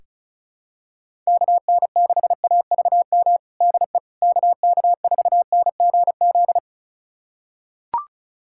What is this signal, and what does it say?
Morse code, heard as a keyed tone, about 36 wpm, 700 Hz. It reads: KN6AVM DE KK4NGZ
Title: USA Callsign DE Callsign 35wpm